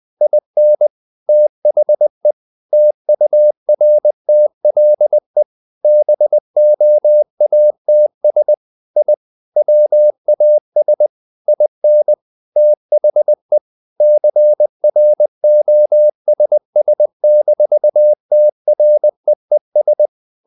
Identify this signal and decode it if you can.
Morse code, heard as a keyed tone, about 20 wpm, 600 Hz. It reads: IN THE TURTLE BOATS I WAS IN THE CROSS-TREES
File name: the old man and the sea part 2 20wpm